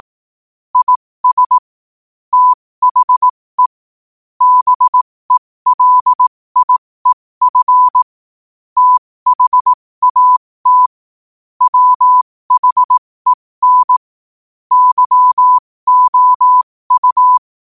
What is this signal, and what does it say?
Morse code, heard as a keyed tone, about 18 wpm, spaced as for 15 wpm, 1000 Hz. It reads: IS THE BELIEF THAT WHEN YOU
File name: qotdfiles / 15wpm